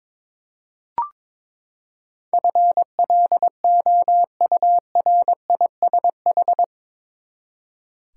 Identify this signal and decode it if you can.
Morse code, heard as a keyed tone, about 21 wpm, 700 Hz. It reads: FLOURISH
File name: practice / 22wpm / Eight Letter Words - 22wpm